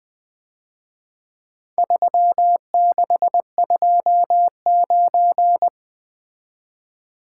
Morse code, heard as a keyed tone, about 23 wpm, 700 Hz. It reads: 3629
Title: Speed-Racing - Four Letter-Number Combinations - 20wpm